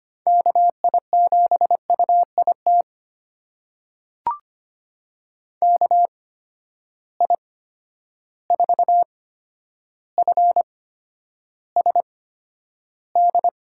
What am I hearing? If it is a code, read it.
Morse code, heard as a keyed tone, about 25 wpm, spaced as for 11 wpm, 700 Hz. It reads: KI7UIT KI4FSD
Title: USA Call Signs - 6 Characters - 8x Character Spacing - 25wpm